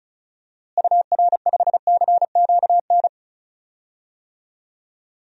Morse code, heard as a keyed tone, about 36 wpm, 700 Hz. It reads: UR5CQN
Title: DXCC-100 Call Signs - 6 Characters - Repeated 3x - 35wpm